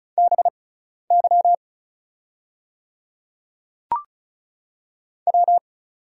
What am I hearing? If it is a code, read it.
Morse code, heard as a keyed tone, about 36 wpm, spaced as for 12 wpm, 690 Hz. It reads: BY W
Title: 6x Character Spacing - Top 300 Words - 35wpm